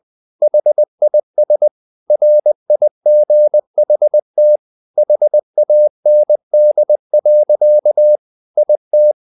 Morse code, heard as a keyed tone, about 20 wpm, 600 Hz. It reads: HIS RIGHT HAND. IT
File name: the old man and the sea part 2 20wpm